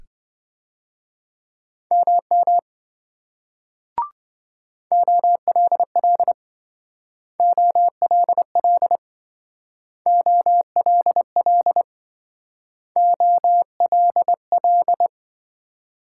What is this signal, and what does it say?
Morse code, heard as a keyed tone, about 26 wpm, 700 Hz. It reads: MM OLL OLL OLL OLL